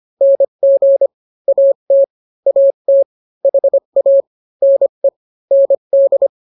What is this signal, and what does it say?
Morse code, heard as a keyed tone, about 25 wpm, 550 Hz. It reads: NG AT AT HA NE ND